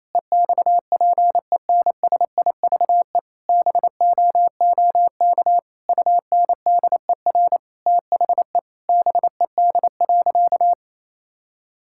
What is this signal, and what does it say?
Morse code, heard as a keyed tone, about 28 wpm, 700 Hz. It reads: EXPENSIVE BOOK UNDER THE BED.
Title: Sentences from Top 1000 Words - Review Part 6 - 28wpm